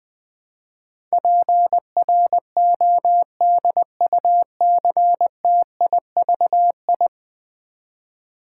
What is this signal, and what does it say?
Morse code, heard as a keyed tone, about 20 wpm, 700 Hz. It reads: PRODUCTIVI
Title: Build A Word - 5 Letters At A Time - 12-Letter Words - 20wpm